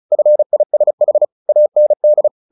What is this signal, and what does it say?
Morse code, heard as a keyed tone, about 34 wpm, 600 Hz. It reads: FISH AND
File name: the old man and the sea 2 35wpm